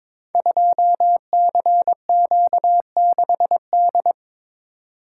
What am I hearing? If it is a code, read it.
Morse code, heard as a keyed tone, about 22 wpm, 700 Hz. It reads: 2CQ6D
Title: Code Groups - 5 Letter:Numbers - 22wpm